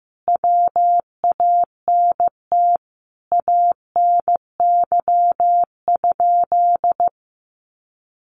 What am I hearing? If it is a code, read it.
Morse code, heard as a keyed tone, about 14 wpm, 700 Hz. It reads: WANT ANY?
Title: Sentences from Top 300 Words 15wpm